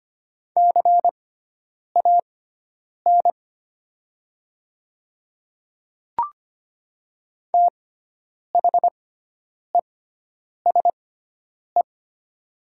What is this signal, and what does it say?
Morse code, heard as a keyed tone, about 25 wpm, spaced as for 9 wpm, 700 Hz. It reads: CAN THESE
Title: 6x Character Spacing - Top 100 Words - 25wpm